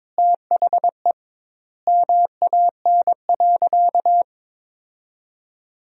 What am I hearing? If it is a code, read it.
Morse code, heard as a keyed tone, about 22 wpm, 700 Hz. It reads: THE MAN.
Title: 2x Word-Spacing - Sentences from Top 400 Words - 22wpm